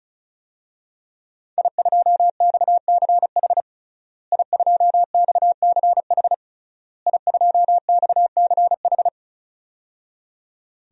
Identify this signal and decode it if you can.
Morse code, heard as a keyed tone, about 36 wpm, 700 Hz. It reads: I2XCH I2XCH I2XCH
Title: DXCC-100 Call Signs - 5 Characters - Repeated 3x - 35wpm